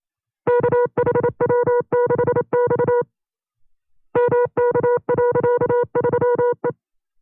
Morse code, heard as a keyed tone, about 27 wpm, 490 Hz. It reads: KHW6X MK.3E
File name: CW-practice/25WPM